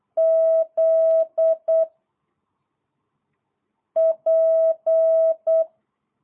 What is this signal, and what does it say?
Morse code, heard as a keyed tone, about 8 wpm, 660 Hz. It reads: Z P